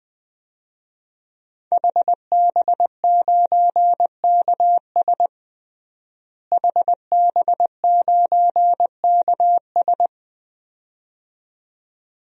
Morse code, heard as a keyed tone, about 21 wpm, 700 Hz. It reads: HB9KS HB9KS
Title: DXCC-100 Call Signs - 5 Characters - Repeated 2x - 20wpm